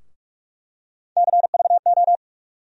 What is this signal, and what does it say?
Morse code, heard as a keyed tone, about 46 wpm, 700 Hz. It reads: CVO
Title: USA Call Sign Postfixes - 3 Letters 45wpm